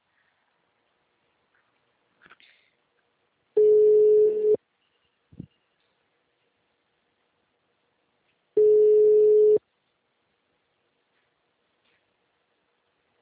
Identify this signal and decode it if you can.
Ringback tone (Europe, 425 Hz).